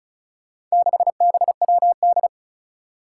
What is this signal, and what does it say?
Morse code, heard as a keyed tone, about 36 wpm, 700 Hz. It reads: 6BWD
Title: Four Challenging Letter-Number Combinations 35wpm